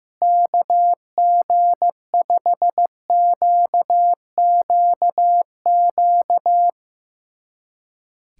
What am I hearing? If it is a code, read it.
Morse code, heard as a keyed tone, about 15 wpm, 700 Hz. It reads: KG5QQQ
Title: USA Callsign DE Callsign - 6 Characters - 15wpm